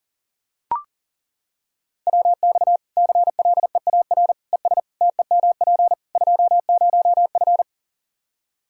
Morse code, heard as a keyed tone, about 40 wpm, 700 Hz. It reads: WX CLEAR ES TEMP 20F
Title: Practice QSO - Exchange 2 - Info - 40wpm